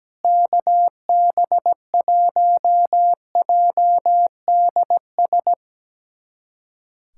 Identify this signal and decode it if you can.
Morse code, heard as a keyed tone, about 17 wpm, 700 Hz. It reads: KB1JDS
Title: USA Callsign DE Callsign 17wpm